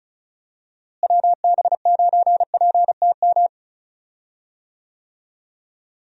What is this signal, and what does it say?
Morse code, heard as a keyed tone, about 34 wpm, 700 Hz. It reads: WB9PTM
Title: USA Call Signs - 6 Characters - Repeated 3x - 35wpm